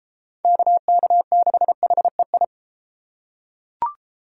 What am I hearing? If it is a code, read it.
Morse code, heard as a keyed tone, about 32 wpm, 700 Hz. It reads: KK6HEI